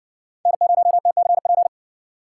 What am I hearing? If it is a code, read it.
Morse code, heard as a keyed tone, about 54 wpm, 700 Hz. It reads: N0TCP